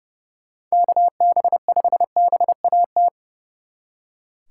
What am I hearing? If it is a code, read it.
Morse code, heard as a keyed tone, about 38 wpm, 700 Hz. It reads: KB5BAT
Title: Speed Racing - USA Call Signs 30wpm